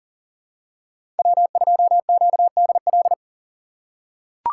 Morse code, heard as a keyed tone, about 40 wpm, 700 Hz. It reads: W2QDL
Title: USA Call Signs - 4x Character Spacing - 40wpm